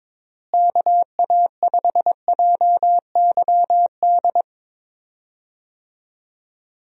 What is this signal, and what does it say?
Morse code, heard as a keyed tone, about 22 wpm, 700 Hz. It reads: KA5JYD
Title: USA Call Signs - 6 Characters - Repeated 2x - 22wpm